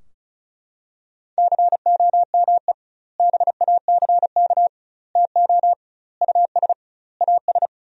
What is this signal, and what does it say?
Morse code, heard as a keyed tone, about 36 wpm, 700 Hz. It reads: COME BACK TO US AS
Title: 2x Word-Spacing - Sentences from Top 500 Words - 35wpm